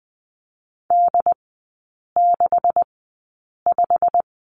Morse code, heard as a keyed tone, about 20 wpm, spaced as for 8 wpm, 700 Hz. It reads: D65
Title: 166 Lesson 27e - Number 7 - Three Characters 20wpm